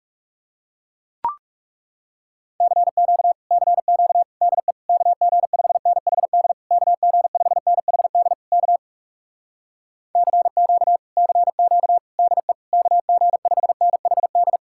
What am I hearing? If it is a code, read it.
Morse code, heard as a keyed tone, about 40 wpm, 700 Hz. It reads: CQ CQ DE KG5NHD KG5NHD K CQ CQ DE KG5NHD